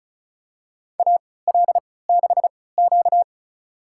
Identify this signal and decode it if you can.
Morse code, heard as a keyed tone, about 36 wpm, spaced as for 20 wpm, 700 Hz. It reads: AL6Q
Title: USA Call Signs - 4 Characters - 3x Character Spacing - 35wpm